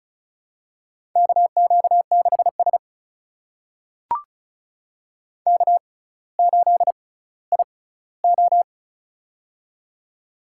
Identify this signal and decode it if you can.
Morse code, heard as a keyed tone, about 36 wpm, spaced as for 18 wpm, 700 Hz. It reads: KQ6S K8IO
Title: USA Call Signs - 4 Characters - 6x Character Spacing - 35wpm